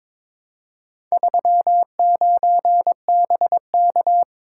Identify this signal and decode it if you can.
Morse code, heard as a keyed tone, about 22 wpm, 700 Hz. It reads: 39BK
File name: practice / 22wpm / Four Letter-Number Combinations - 22wpm